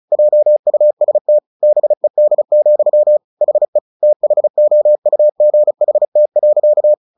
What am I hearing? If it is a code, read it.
Morse code, heard as a keyed tone, about 35 wpm, 600 Hz. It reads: JUST BED, HE THOUGHT.
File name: the old man and the sea 2 35wpm